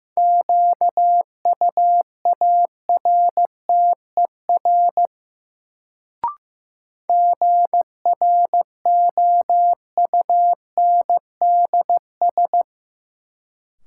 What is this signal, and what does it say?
Morse code, heard as a keyed tone, about 15 wpm, 700 Hz. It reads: QUARTER GROUNDS
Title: Sentences from Top 1300 Words - New Word Drill Alt - 15wpm